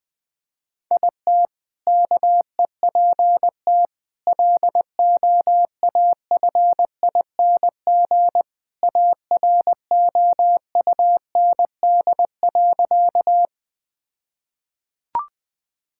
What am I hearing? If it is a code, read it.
Morse code, heard as a keyed tone, about 20 wpm, 700 Hz. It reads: IT KEPT LOAFING AROUND.
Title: All-Age Appropriate Jokes - 20wpm